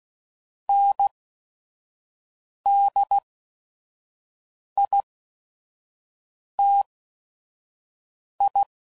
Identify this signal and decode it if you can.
Morse code, heard as a keyed tone, about 16 wpm, spaced as for 5 wpm, 800 Hz. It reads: NDITI
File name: CW Code Groups @5WPM